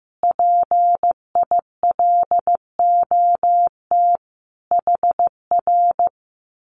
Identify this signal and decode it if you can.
Morse code, heard as a keyed tone, about 15 wpm, 700 Hz. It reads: PILOT HR
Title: Common QSO Phrases Encore 15wpm